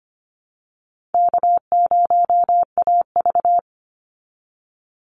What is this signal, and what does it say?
Morse code, heard as a keyed tone, about 25 wpm, 700 Hz. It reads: K0AV
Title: Easy Call Signs Repeated 3x - 25wpm